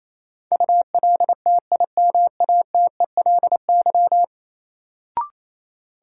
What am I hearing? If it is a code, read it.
Morse code, heard as a keyed tone, about 28 wpm, 700 Hz. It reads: ULTIMATELY
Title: Sentences from Top 1200 Words - New Word Drill - 28wpm